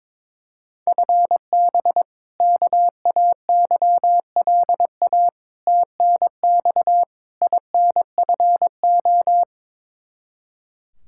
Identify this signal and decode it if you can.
Morse code, heard as a keyed tone, about 22 wpm, 700 Hz. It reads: FB KAYLA TNX INFO
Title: K1USN SST QSO Practice 22wpm